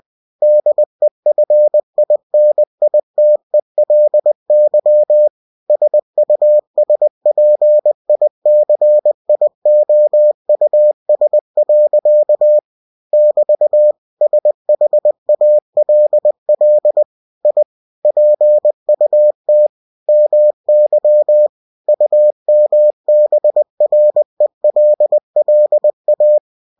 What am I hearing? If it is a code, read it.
Morse code, heard as a keyed tone, about 20 wpm, 600 Hz. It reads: DEFINITELY SUSPICIOUS. <BT> SHALL I PUT MY UMBRELLA